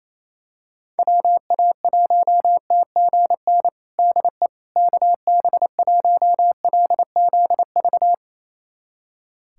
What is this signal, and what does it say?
Morse code, heard as a keyed tone, about 27 wpm, 700 Hz. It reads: WA1TGN DE KB1LZV
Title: USA Callsign DE Callsign - 6 Characters - 28wpm